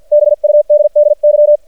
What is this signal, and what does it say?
Morse code, heard as a keyed tone, about 50 wpm, spaced as for 43 wpm, 600 Hz. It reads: BUDDY
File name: archived / CRSnet